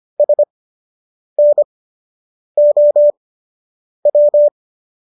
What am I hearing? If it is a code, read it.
Morse code, heard as a keyed tone, about 25 wpm, spaced as for 8 wpm, 590 Hz. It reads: SNOW